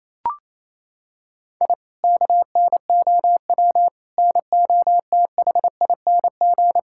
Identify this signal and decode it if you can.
Morse code, heard as a keyed tone, about 28 wpm, 700 Hz. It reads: I KNOW NOTHING
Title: Sentences from Top 1000 Words - Review Part 6 - 28wpm